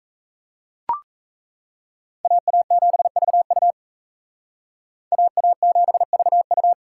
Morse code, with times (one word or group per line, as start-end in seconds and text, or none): 2.2-3.7 AA7VU
5.1-6.7 AA7VU